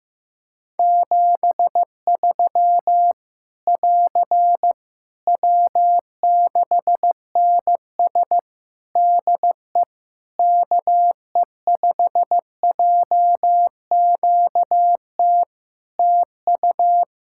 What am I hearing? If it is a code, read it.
Morse code, heard as a keyed tone, about 15 wpm, 700 Hz. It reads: 73 <AR> W6NS DE KE5JQT TU
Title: Practice QSO - Exchange 4 - Ending - 15wpm